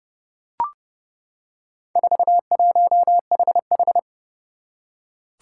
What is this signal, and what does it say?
Morse code, heard as a keyed tone, about 29 wpm, 690 Hz. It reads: 41HH
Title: Four Challenging Letter-Number Combinations 30wpm